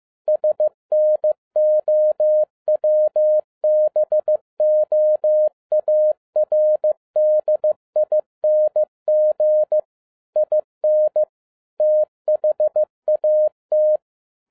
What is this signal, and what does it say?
Morse code, heard as a keyed tone, about 15 wpm, 610 Hz. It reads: SNOWBOARDING IN THAT
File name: FIELD DAY CONTEST QRV FROM SAKAE VILLAGE 15wpm